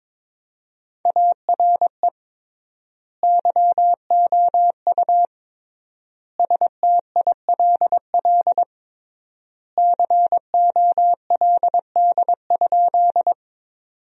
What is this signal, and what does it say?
Morse code, heard as a keyed tone, about 21 wpm, 700 Hz. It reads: ARE YOU STILL COLD?
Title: 3x Word-Spacing - Sentences from Top 400 Words - 22wpm